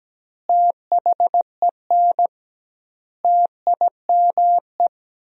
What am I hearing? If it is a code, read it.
Morse code, heard as a keyed tone, about 17 wpm, 700 Hz. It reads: THEN TIME
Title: 2x Word-Spacing - Sentences from Top 300 Words - 17wpm